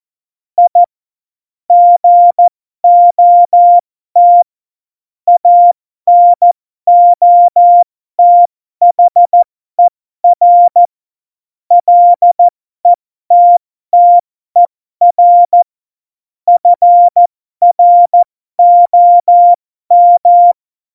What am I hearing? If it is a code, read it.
Morse code, heard as a keyed tone, about 14 wpm, spaced as for 12 wpm, 700 Hz. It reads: I GOT ANOTHER LETTER FROM
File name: Morsekurs / Witze